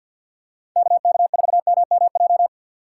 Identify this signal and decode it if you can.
Morse code, heard as a keyed tone, about 51 wpm, 700 Hz. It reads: KK4MMJ
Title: USA Call Signs - 6 Characters - 6x Character Spacing - 50wpm